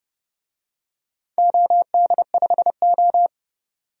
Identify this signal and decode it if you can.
Morse code, heard as a keyed tone, about 32 wpm, 700 Hz. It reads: OD5O
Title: DXCC-100 Call Signs - 4 Characters - 30wpm